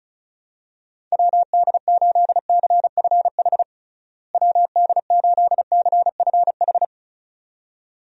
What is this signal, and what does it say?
Morse code, heard as a keyed tone, about 34 wpm, 700 Hz. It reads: WD8CFH WD8CFH
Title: USA Call Signs - 6 Characters - Repeated 2x - 35wpm